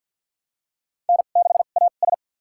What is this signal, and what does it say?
Morse code, heard as a keyed tone, about 52 wpm, spaced as for 37 wpm, 700 Hz. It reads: N6AS
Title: USA Call Signs - 4 Characters - 2x Character Spacing - 50wpm